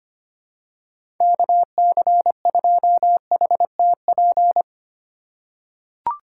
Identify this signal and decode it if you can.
Morse code, heard as a keyed tone, about 25 wpm, 700 Hz. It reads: KC2HTP
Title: USA Call Signs - 6 Characters - 2x Character Spacing - 25wpm